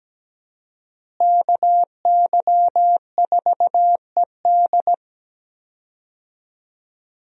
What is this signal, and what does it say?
Morse code, heard as a keyed tone, about 17 wpm, 700 Hz. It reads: KY4ED
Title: USA Call Signs - 5 Characters - Repeated 2x - 17wpm